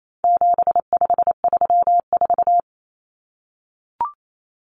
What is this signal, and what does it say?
Morse code, heard as a keyed tone, about 28 wpm, 700 Hz. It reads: 7534
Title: Common Cut Numbers - 4-Digit 28wpm